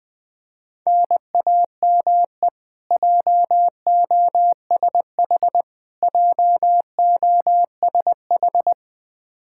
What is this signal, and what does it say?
Morse code, heard as a keyed tone, about 21 wpm, 700 Hz. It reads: NAME JOSH JOSH
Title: Practice QSO - Exchange 1 - Beginning - 20wpm